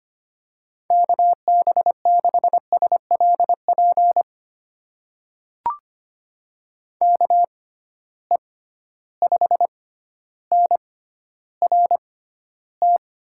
Morse code, heard as a keyed tone, about 25 wpm, spaced as for 13 wpm, 700 Hz. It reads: KB6SLP KE5NRT
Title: USA Call Signs - 6 Characters - 6x Character Spacing - 25wpm